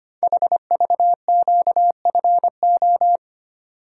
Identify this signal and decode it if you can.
Morse code, heard as a keyed tone, about 25 wpm, 700 Hz. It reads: HVQFO
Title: Code Groups - 5 Letters - 25wpm